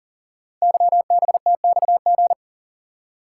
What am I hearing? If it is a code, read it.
Morse code, heard as a keyed tone, about 39 wpm, 690 Hz. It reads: YBTXG
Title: Code Groups - 5 Letters - 40wpm